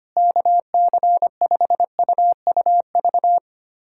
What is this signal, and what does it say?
Morse code, heard as a keyed tone, about 25 wpm, 700 Hz. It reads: KC5UUV
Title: USA Callsign DE Callsign - 6 Characters - 25wpm